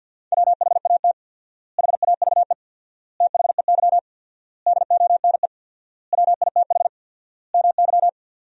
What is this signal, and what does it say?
Morse code, heard as a keyed tone, about 53 wpm, 700 Hz. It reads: WHAT HAVE THEY DONE WITH MY
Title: 4x Word-Spacing - Sentences from Top 500 Words - 50wpm